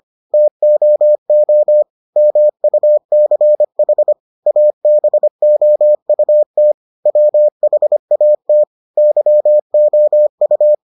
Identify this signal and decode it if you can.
Morse code, heard as a keyed tone, about 25 wpm, 600 Hz. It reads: TOO MUCH ABOUT WHAT YOU